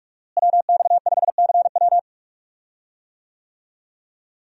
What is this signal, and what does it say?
Morse code, heard as a keyed tone, about 44 wpm, 700 Hz. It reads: WX5CW
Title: USA Call Signs - 5 Characters - Repeated 2x - 45wpm